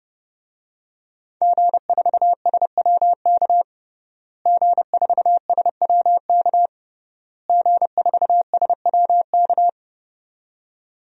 Morse code, heard as a keyed tone, about 32 wpm, 700 Hz. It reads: G4SWK G4SWK G4SWK
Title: DXCC-100 Call Signs - 5 Characters - Repeated 3x - 30wpm